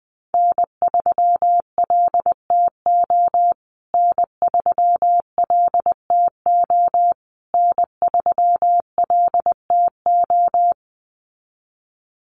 Morse code, heard as a keyed tone, about 20 wpm, 700 Hz. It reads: N3LTO N3LTO N3LTO
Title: USA Call Signs Repeated 3x 20wpm